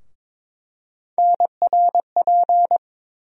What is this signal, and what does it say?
Morse code, heard as a keyed tone, about 21 wpm, 700 Hz. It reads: NRP
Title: USA Call Sign Postfixes - 3 Letters 22wpm